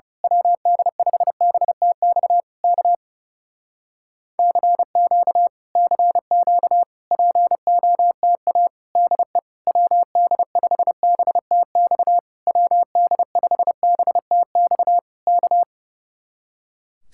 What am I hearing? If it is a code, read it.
Morse code, heard as a keyed tone, about 29 wpm, 700 Hz. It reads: WD5BTX K CQ CQ POTA DE WD5BTX WD5BTX K
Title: Speed-Racing Common QSO Phrases - 30wpm